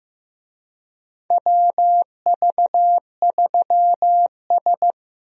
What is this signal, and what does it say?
Morse code, heard as a keyed tone, about 15 wpm, 700 Hz. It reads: WV3S
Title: USA Call Signs - 4 Characters - Repeated 3x - 15wpm